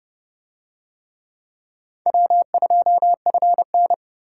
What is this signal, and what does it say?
Morse code, heard as a keyed tone, about 30 wpm, 700 Hz. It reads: W2FN